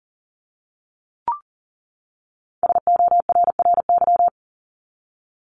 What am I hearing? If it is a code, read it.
Morse code, heard as a keyed tone, about 40 wpm, 700 Hz. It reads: SORRY
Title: Top 1751-2000 Words 40wpm